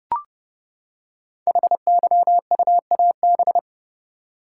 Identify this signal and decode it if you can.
Morse code, heard as a keyed tone, about 29 wpm, 700 Hz. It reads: HYUAB
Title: Code Groups - 5 Letter:Numbers - 30wpm